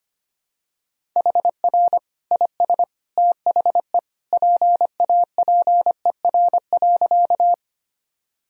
Morse code, heard as a keyed tone, about 25 wpm, 700 Hz. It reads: HR IS THE PAPER.
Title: Sentences from Top 300 Words and Abbreviations 25wpm